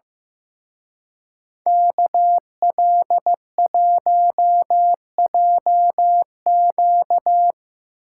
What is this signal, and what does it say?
Morse code, heard as a keyed tone, about 17 wpm, 700 Hz. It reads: KL1JQ
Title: Speed Racing - USA 2x2 Call Signs 15wpm